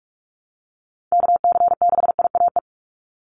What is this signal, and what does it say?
Morse code, heard as a keyed tone, about 44 wpm, 700 Hz. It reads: KC6IAE
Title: USA Call Signs 45wpm